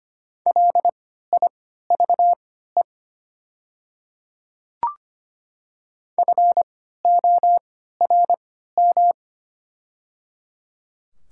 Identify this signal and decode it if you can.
Morse code, heard as a keyed tone, about 25 wpm, spaced as for 14 wpm, 700 Hz. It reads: LIVE FORM
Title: 3x Character Spacing - Top 500 Words - 25wpm